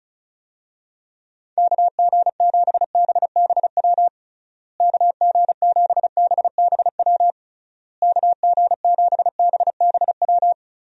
Morse code, heard as a keyed tone, about 34 wpm, 700 Hz. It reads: KG7BBW KG7BBW KG7BBW
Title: USA Call Signs - 6 Characters - Repeated 3x - 35wpm